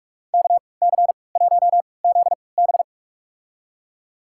Morse code, heard as a keyed tone, about 42 wpm, spaced as for 25 wpm, 700 Hz. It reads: KC1ZB
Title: USA Call Signs - 5 Characters - 3x Character Spacing - 45wpm